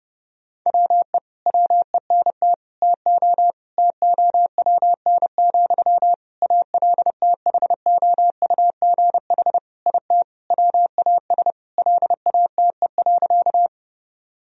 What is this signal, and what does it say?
Morse code, heard as a keyed tone, about 30 wpm, 700 Hz. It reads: WE WENT TO TOWN, ALTHOUGH IT WAS LATE.
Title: Sentences from Top 1000 Words - Review Part 3 - 30wpm